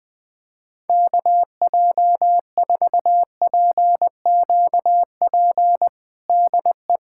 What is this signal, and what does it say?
Morse code, heard as a keyed tone, about 21 wpm, 700 Hz. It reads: KJ4PQP DE
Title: USA Callsign DE Callsign - 6 Characters - 20wpm